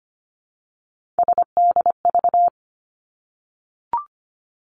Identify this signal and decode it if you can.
Morse code, heard as a keyed tone, about 25 wpm, 700 Hz. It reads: SDV
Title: Three Challenging Letter-Number Combinations 25wpm